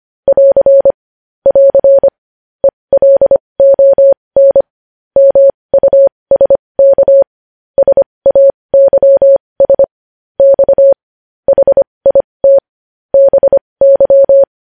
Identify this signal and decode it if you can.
Morse code, heard as a keyed tone, about 25 wpm, spaced as for 20 wpm, 550 Hz. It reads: <AR> <AR> ELON MUSK SAYS X HIT BY
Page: News Headlines in Morse Code at 20 WPM